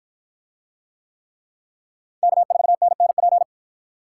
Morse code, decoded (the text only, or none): K4NNP